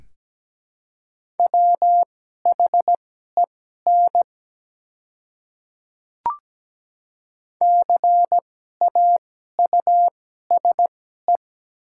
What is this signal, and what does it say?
Morse code, heard as a keyed tone, about 17 wpm, spaced as for 12 wpm, 700 Hz. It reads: WHEN CAUSE
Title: 2x Character Spacing - Top 200 Words - 17wpm